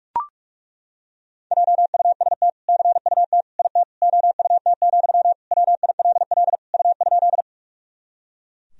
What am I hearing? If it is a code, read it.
Morse code, heard as a keyed tone, about 43 wpm, 700 Hz. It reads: JUST CUT IT OUT, WILL U?
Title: Sentences from Top 300 Words and Abbreviations 45wpm